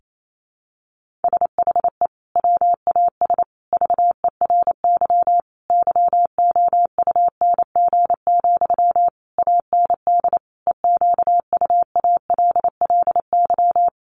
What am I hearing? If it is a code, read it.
Morse code, heard as a keyed tone, about 28 wpm, 700 Hz. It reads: SHE WAS VERY YOUNG, AND EQUALLY
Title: Sentences from Top 1400 Words 28wpm